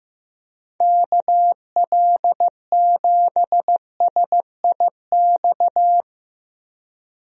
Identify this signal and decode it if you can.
Morse code, heard as a keyed tone, about 15 wpm, 700 Hz. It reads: KL7SIX
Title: DXCC-100 Call Signs - 6 Characters - 15wpm